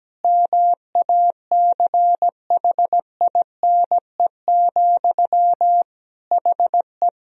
Morse code, heard as a keyed tone, about 17 wpm, 700 Hz. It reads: MACHINE, HE
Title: Sentences from Top 1000 Words - Review Part 2 - 17wpm